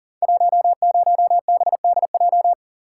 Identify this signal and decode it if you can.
Morse code, heard as a keyed tone, about 42 wpm, 690 Hz. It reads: 10BDJ
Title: Code Groups - 5 Letter:Numbers - 40wpm